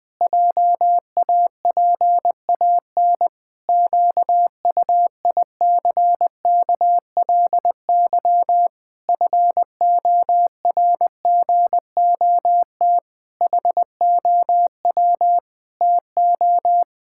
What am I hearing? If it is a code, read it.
Morse code, heard as a keyed tone, about 20 wpm, 700 Hz. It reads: JAPAN QUICKLY FORGOT HOW TO
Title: Pangram Challenge v2 - 20wpm